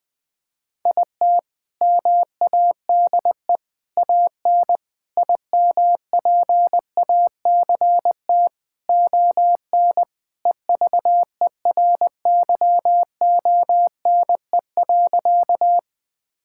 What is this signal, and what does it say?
Morse code, heard as a keyed tone, about 20 wpm, 700 Hz. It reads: IT MADE AN IMPACT ON EVERYONE.
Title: Sentences from Top 1000 Words - Review Part 6 - 20wpm